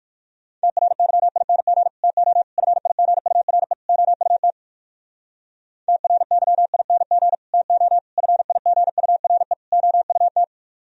Morse code, heard as a keyed tone, about 58 wpm, 700 Hz. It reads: TRYING TO FIGURE OUT TRYING TO FIGURE OUT